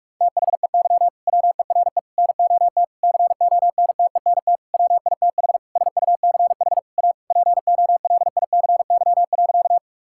Morse code, heard as a keyed tone, about 41 wpm, 700 Hz. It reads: THEY WERE NOT CONTENT WITH SUCH A POLICY.
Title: Sentences from Top 1000 Words - Review Part 6 - 45wpm